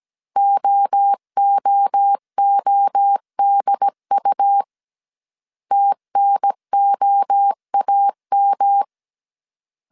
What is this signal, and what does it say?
Morse code, heard as a keyed tone, about 17 wpm, 800 Hz. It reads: OOODU TNOAM